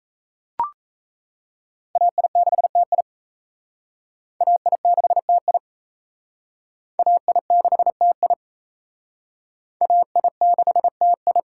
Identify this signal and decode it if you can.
Morse code, heard as a keyed tone, about 36 wpm, 700 Hz. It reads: AI6TI AI6TI AI6TI AI6TI